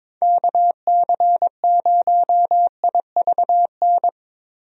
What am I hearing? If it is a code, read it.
Morse code, heard as a keyed tone, about 21 wpm, 700 Hz. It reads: KC0IVN
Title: USA Call Signs - 6 Characters - Repeated 2x - 22wpm